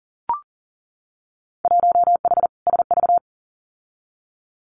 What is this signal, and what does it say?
Morse code, heard as a keyed tone, about 41 wpm, 700 Hz. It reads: 1H SV
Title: Winter Field Day Exchanges 40wpm